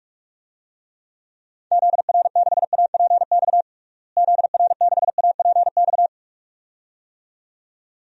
Morse code, heard as a keyed tone, about 46 wpm, 700 Hz. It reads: ZR6APX ZR6APX
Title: DXCC-100 Call Signs - 6 Characters - Repeated 2x - 45wpm